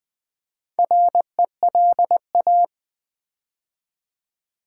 Morse code, heard as a keyed tone, about 20 wpm, 700 Hz. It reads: RELA
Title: Build A Word - 2 Letters At A Time - 6-Letter Words - 20wpm